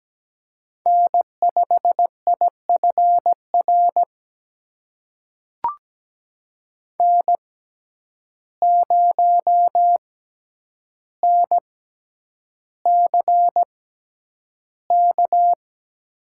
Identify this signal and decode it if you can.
Morse code, heard as a keyed tone, about 17 wpm, spaced as for 9 wpm, 700 Hz. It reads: N5IFR N0NCK
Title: USA Call Signs - 6x Character Spacing - 17wpm